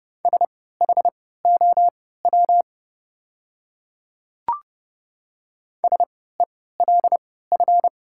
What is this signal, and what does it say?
Morse code, heard as a keyed tone, about 29 wpm, spaced as for 17 wpm, 700 Hz. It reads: SHOW SELF
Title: 3x Character Spacing - Top 500 Words - 30wpm